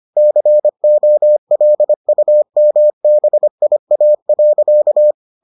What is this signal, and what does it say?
Morse code, heard as a keyed tone, about 25 wpm, 600 Hz. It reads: COLUMBIA.